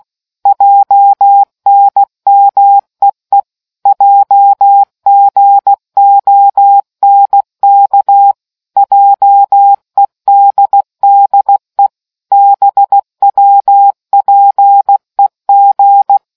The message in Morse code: JNMEE JGONK JEDDE BWPEG

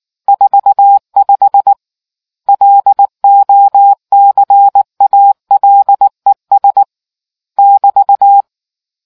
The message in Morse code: 45 LOCALES <BT>

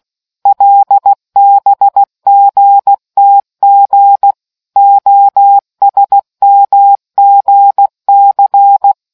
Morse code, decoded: LBGTG OSMGC

